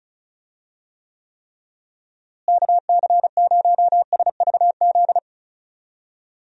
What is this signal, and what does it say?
Morse code, heard as a keyed tone, about 36 wpm, 700 Hz. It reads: KC0SVZ
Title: USA Call Signs - 6 Characters - Repeated 3x - 35wpm